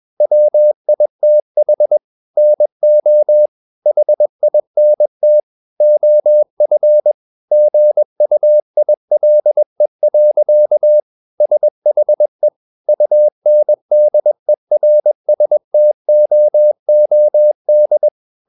Morse code, heard as a keyed tone, about 21 wpm, 600 Hz. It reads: WITH NO HINT OF GUILE. SHE UNDERSTOOD
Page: CW med Gnister